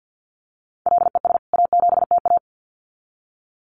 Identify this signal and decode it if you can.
Morse code, heard as a keyed tone, about 50 wpm, 700 Hz. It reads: LES A7TA
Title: CWT Exchange 50wpm